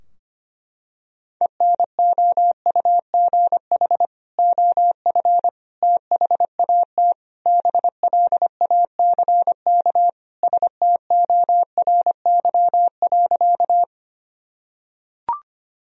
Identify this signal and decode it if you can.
Morse code, heard as a keyed tone, about 25 wpm, 700 Hz. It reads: ENOUGH OF THAT BLACK STORY.
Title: Sentences from Top 1000 Words - Review Part 2 - 25wpm